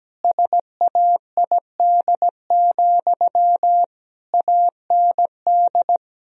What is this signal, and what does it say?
Morse code, heard as a keyed tone, about 17 wpm, 700 Hz. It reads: SAID, AND
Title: Sentences from Top 1000 Words - Review Part 2 - 17wpm